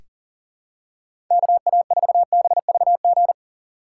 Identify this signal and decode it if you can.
Morse code, heard as a keyed tone, about 40 wpm, 700 Hz. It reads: KA4BVG
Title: K1USN SST QSO Practice - 40wpm